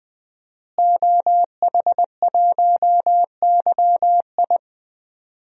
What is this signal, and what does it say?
Morse code, heard as a keyed tone, about 20 wpm, 700 Hz. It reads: OH1YI
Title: DXCC-100 Call Signs - 5 Characters - Repeated 2x - 20wpm